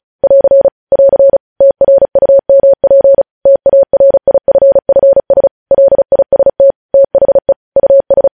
Morse code, heard as a keyed tone, about 36 wpm, 550 Hz. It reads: <AR> <AR> TRUMP TARIFFS LIST THE US
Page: news headlines in morse code at 35 wpm